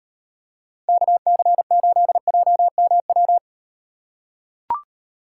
Morse code, heard as a keyed tone, about 38 wpm, 700 Hz. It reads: KC8JMW